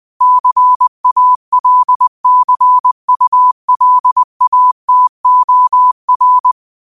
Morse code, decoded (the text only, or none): CALCULATOR